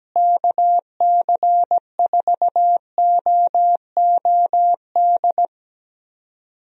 Morse code, heard as a keyed tone, about 17 wpm, 700 Hz. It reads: KC4OOD
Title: USA Call Signs - 6 Characters - Repeated 3x - 17wpm